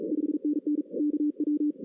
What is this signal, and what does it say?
Morse code, heard as a keyed tone, about 33 wpm, 300 Hz. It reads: 5NN KW
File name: 2016 arrl dx cw / 2016-ARRL-DX-CW-K5TR- - 0-3